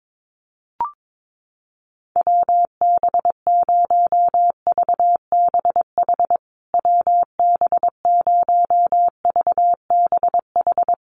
Morse code, heard as a keyed tone, about 22 wpm, 700 Hz. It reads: WB0VBH WB0VBH
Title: USA Call Signs Repeated 3x 22wpm